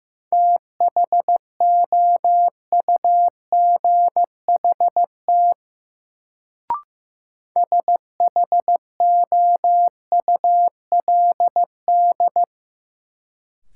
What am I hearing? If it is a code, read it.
Morse code, heard as a keyed tone, about 15 wpm, 700 Hz. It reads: THOUGHT SHOULD
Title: Sentences from Top 300 Words - New Word Drill - 15wpm